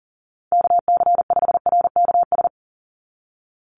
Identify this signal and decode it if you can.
Morse code, heard as a keyed tone, about 40 wpm, 700 Hz. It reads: KC5RKS
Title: USA Call Signs 40wpm